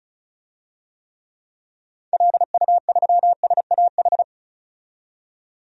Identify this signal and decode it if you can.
Morse code, heard as a keyed tone, about 34 wpm, 700 Hz. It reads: LU3SAH
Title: DXCC-100 Call Signs - 6 Characters - 35wpm